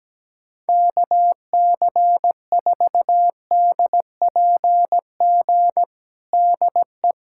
Morse code, heard as a keyed tone, about 17 wpm, 700 Hz. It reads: KC4DPG DE
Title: USA Callsign DE Callsign - 6 Characters - 17wpm